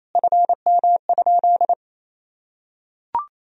Morse code, heard as a keyed tone, about 28 wpm, 700 Hz. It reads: FM?
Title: Sentences from Top 200 Words and Abbreviations 28wpm